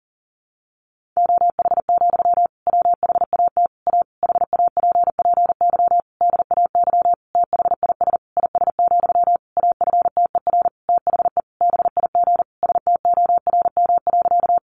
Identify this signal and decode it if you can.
Morse code, heard as a keyed tone, about 40 wpm, 700 Hz. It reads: OH, WHAT A HAPPY DAY THIS IS, AFTER THE BIG STORM.
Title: Sentences from Top 1500 Words 40wpm